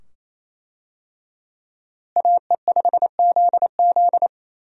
Morse code, heard as a keyed tone, about 28 wpm, 700 Hz. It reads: AE5ZZ
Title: USA Call Signs - 5 Characters - Repeated 2x - 28wpm